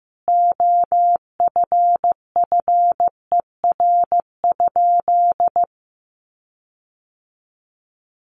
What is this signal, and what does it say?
Morse code, heard as a keyed tone, about 14 wpm, 700 Hz. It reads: OFFER?
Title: Sentences from Top 1300 Words 15wpm